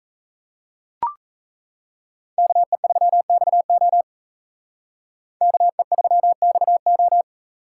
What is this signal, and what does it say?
Morse code, heard as a keyed tone, about 34 wpm, 700 Hz. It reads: KE3XO KE3XO
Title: Speed Racing - USA 2x2 Call Signs 28wpm